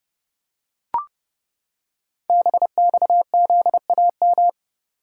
Morse code, heard as a keyed tone, about 28 wpm, 700 Hz. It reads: BXZAM
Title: Code Groups - 5 Letters - 30wpm